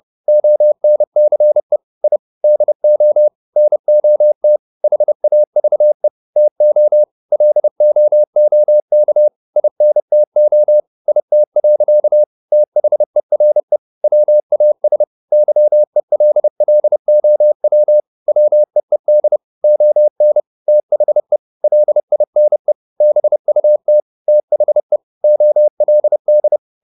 Morse code, heard as a keyed tone, about 30 wpm, 600 Hz. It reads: ONCE I DO NOT HAVE TO LOOK INTO IT. THERE WAS YELLOW WEED ON THE LINE BUT THE OLD
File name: the old man and the sea / the old man and the sea 1 30wpm